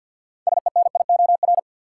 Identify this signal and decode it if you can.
Morse code, heard as a keyed tone, about 53 wpm, 700 Hz. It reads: SENIOR
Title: Six Letter Words - 50wpm